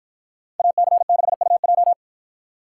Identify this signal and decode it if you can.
Morse code, heard as a keyed tone, about 53 wpm, 700 Hz. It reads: AC6UJ